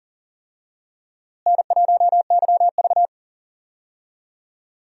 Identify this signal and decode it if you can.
Morse code, heard as a keyed tone, about 40 wpm, 700 Hz. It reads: N1YV
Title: USA Call Signs - 4 Characters - Repeated 2x - 40wpm